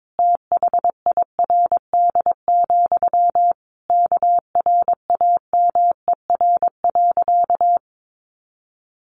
Morse code, heard as a keyed tone, about 22 wpm, 700 Hz. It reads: THIRD, KRAMER.
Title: A Sound Of Thunder 22wpm